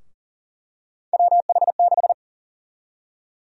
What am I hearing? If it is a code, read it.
Morse code, heard as a keyed tone, about 40 wpm, 700 Hz. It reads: WH6
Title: USA Call Sign Prefixes - Repeated 2x - 40wpm